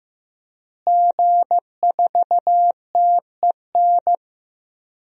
Morse code, heard as a keyed tone, about 15 wpm, 700 Hz. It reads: G4TEN
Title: DXCC-100 Call Signs - 5 Characters - Repeated 2x - 15wpm